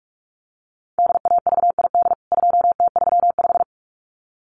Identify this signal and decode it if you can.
Morse code, heard as a keyed tone, about 44 wpm, 700 Hz. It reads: DAVID 2T35
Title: CWT Exchange 45wpm